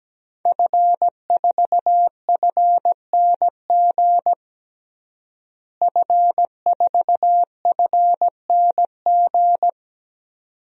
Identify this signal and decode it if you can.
Morse code, heard as a keyed tone, about 17 wpm, 700 Hz. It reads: F4FNG F4FNG
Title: DXCC-100 Call Signs - 5 Characters - Repeated 2x - 17wpm